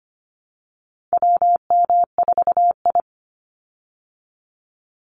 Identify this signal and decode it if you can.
Morse code, heard as a keyed tone, about 25 wpm, 700 Hz. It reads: WM4I
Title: Call Signs Repeated 3x - 25wpm